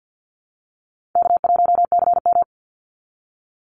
Morse code, heard as a keyed tone, about 48 wpm, 700 Hz. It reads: K1CM
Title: NA SOTA Chaser Callsigns 50wpm